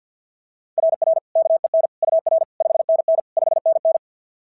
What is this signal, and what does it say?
Morse code, heard as a keyed tone, about 49 wpm, 690 Hz. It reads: RR KEN UR 5NN 5NN
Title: SOTA QSO Practice 50wpm